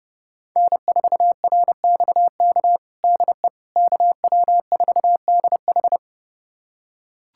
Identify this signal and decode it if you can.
Morse code, heard as a keyed tone, about 30 wpm, 700 Hz. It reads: N4RXK DE KW4DH
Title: USA Callsign DE Callsign - 5 Characters - 30wpm